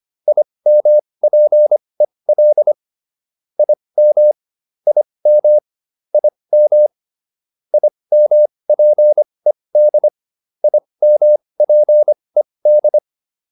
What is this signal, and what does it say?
Morse code, heard as a keyed tone, about 25 wpm, spaced as for 20 wpm, 600 Hz. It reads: IMPEL IM IM IM IMPED IMPED
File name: IM5 20